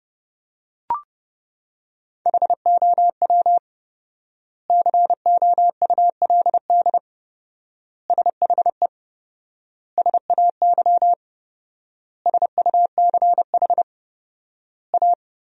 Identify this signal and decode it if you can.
Morse code, heard as a keyed tone, about 30 wpm, 700 Hz. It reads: HOW COULD SHE SAY SUCH A
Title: 4x Word-Spacing - Sentences from Top 200 Words - 30wpm